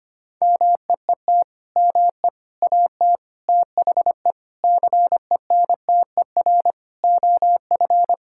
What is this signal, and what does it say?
Morse code, heard as a keyed tone, about 25 wpm, 700 Hz. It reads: MEET ME AT THE CENTER OF
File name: practice / 25wpm / Sentences with 3-Word Phrases - 25wpm